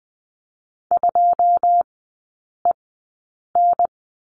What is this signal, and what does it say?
Morse code, heard as a keyed tone, about 20 wpm, spaced as for 8 wpm, 700 Hz. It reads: 2EN
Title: 087 Lesson 15e - Number 5 - Three Characters 20wpm